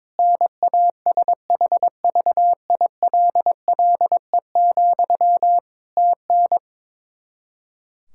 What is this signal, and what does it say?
Morse code, heard as a keyed tone, about 22 wpm, 700 Hz. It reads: NASHVILLE, TN
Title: Practice QSO - Exchange 1 - Beginning - 22wpm